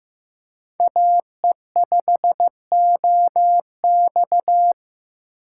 Morse code, heard as a keyed tone, about 15 wpm, 700 Hz. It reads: AE5OX
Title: USA Call Signs - 6x Character Spacing - 15wpm